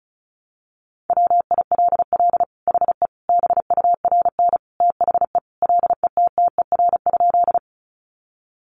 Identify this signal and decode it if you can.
Morse code, heard as a keyed tone, about 34 wpm, 700 Hz. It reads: WILL HE BURN THE LETTER?
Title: Sentences from Top 1300 Words 35wpm